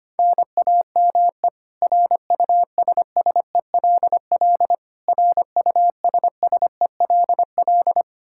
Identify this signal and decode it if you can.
Morse code, heard as a keyed tone, about 25 wpm, 700 Hz. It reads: NAME RUSSELL RUSSELL
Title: K1USN SST QSO Practice 25wpm